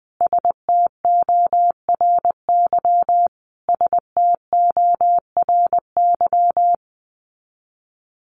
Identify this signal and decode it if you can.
Morse code, heard as a keyed tone, about 20 wpm, 700 Hz. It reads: STORY STORY
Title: Top 500 Words Repeated 20wpm